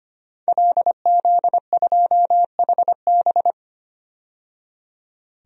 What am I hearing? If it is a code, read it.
Morse code, heard as a keyed tone, about 25 wpm, 700 Hz. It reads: LZ2HB
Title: DXCC-100 Call Signs - 5 Characters - 25wpm